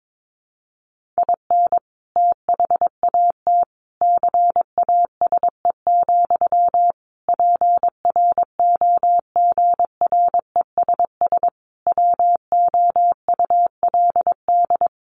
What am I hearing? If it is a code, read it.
Morse code, heard as a keyed tone, about 22 wpm, 700 Hz. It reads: IN THAT CASE, PROGRESS WOULD
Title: Sentences from Top 1400 Words 22wpm